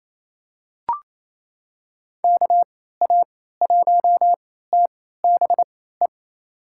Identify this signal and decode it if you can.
Morse code, heard as a keyed tone, about 27 wpm, spaced as for 16 wpm, 700 Hz. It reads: KA1TBE
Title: USA Call Signs - 6 Characters - 3x Character Spacing - 28wpm